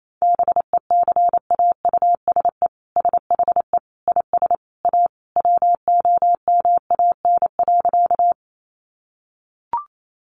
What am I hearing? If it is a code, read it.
Morse code, heard as a keyed tone, about 28 wpm, 700 Hz. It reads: BECAUSE SHE IS A WOMAN.
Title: Sentences from Top 1100 Words 28wpm